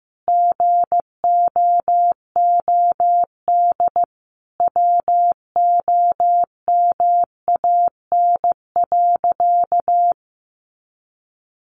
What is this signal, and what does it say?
Morse code, heard as a keyed tone, about 15 wpm, 700 Hz. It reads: GOOD WOMAN.
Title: Sentences from Top 1100 Words 15wpm